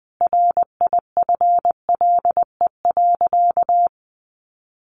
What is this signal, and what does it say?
Morse code, heard as a keyed tone, about 20 wpm, 700 Hz. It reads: RIFLE.
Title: A Sound Of Thunder 20wpm